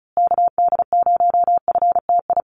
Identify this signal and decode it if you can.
Morse code, heard as a keyed tone, about 34 wpm, 700 Hz. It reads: KD0FTI
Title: USA Call Signs Repeated 3x 35wpm